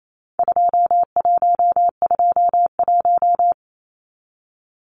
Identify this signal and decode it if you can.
Morse code, heard as a keyed tone, about 28 wpm, 700 Hz. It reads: 2121
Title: Numbers - 4-Digit 28wpm